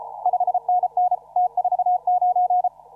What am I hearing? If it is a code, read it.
Morse code, heard as a keyed tone, about 33 wpm, 720 Hz. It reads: 5NN T49
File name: CW / RHOB CW 2017 HA6P 0001